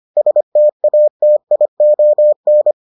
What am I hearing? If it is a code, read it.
Morse code, heard as a keyed tone, about 25 wpm, 600 Hz. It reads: STATION